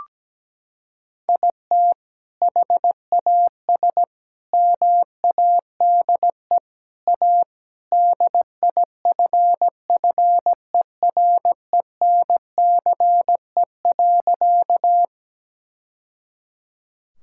Morse code, heard as a keyed tone, about 17 wpm, 700 Hz. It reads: IT HAS MADE A DIFFERENCE.
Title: Sentences from Top 1000 Words - Review Part 5 - 17wpm